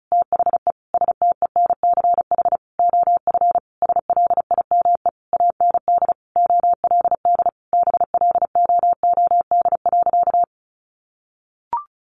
Morse code, heard as a keyed tone, about 34 wpm, 700 Hz. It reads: THE STENCH OF SLIME AND OLD BLOOD.
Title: A Sound Of Thunder 35wpm